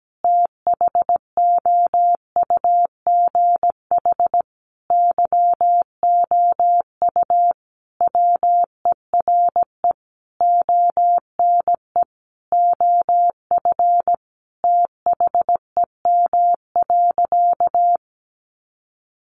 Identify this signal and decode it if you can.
Morse code, heard as a keyed tone, about 17 wpm, 700 Hz. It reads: THOUGH YOU WERE ONE OF THEM.
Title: Sentences from Top 1500 Words 17wpm